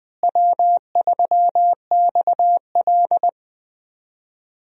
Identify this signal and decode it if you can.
Morse code, heard as a keyed tone, about 20 wpm, 700 Hz. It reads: W3XL
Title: USA Call Signs - 4 Characters - Repeated 3x - 20wpm